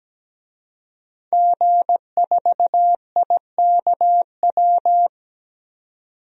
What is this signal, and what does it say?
Morse code, heard as a keyed tone, about 17 wpm, 700 Hz. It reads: G4IKW
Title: DXCC-100 Call Signs - 5 Characters - 17wpm